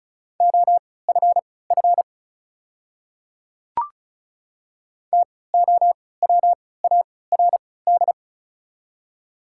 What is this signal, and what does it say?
Morse code, heard as a keyed tone, about 36 wpm, spaced as for 20 wpm, 690 Hz. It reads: OFF TOWARD
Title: 3x Character Spacing - Top 500 Words - 35wpm